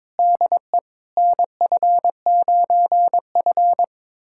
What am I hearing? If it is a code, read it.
Morse code, heard as a keyed tone, about 22 wpm, 700 Hz. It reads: DE NF9F
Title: USA Callsign DE Callsign - 4 Characters - 22wpm